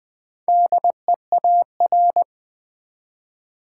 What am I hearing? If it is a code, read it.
Morse code, heard as a keyed tone, about 20 wpm, 700 Hz. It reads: DEAR
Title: Four Letter Words - 20wpm